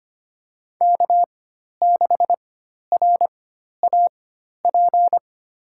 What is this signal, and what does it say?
Morse code, heard as a keyed tone, about 25 wpm, spaced as for 12 wpm, 700 Hz. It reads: K6RAP
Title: USA Call Signs - 5 Characters - 4x Character Spacing - 25wpm